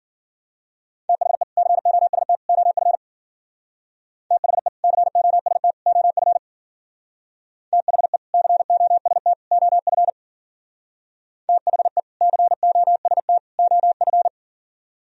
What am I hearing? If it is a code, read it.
Morse code, heard as a keyed tone, about 51 wpm, 700 Hz. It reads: THE COST OF THE COST OF THE COST OF THE COST OF